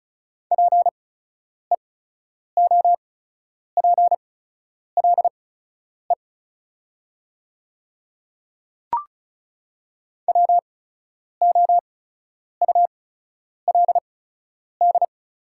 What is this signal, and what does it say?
Morse code, heard as a keyed tone, about 34 wpm, spaced as for 10 wpm, 700 Hz. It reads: PEOPLE WOULD
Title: 8x Character Spacing - Top 200 Words - 35wpm